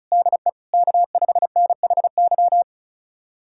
Morse code, heard as a keyed tone, about 36 wpm, 700 Hz. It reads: DE K5NHY
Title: USA Callsign DE Callsign - 5 Characters - 35wpm